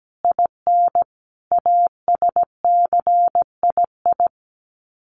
Morse code, heard as a keyed tone, about 17 wpm, 700 Hz. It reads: IN ASCII
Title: Sets of 5 Words 17wpm